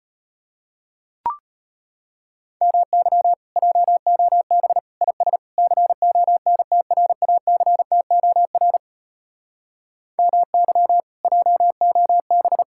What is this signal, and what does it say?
Morse code, heard as a keyed tone, about 36 wpm, 700 Hz. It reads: MY JOB IS CONTRACTOR MY JOB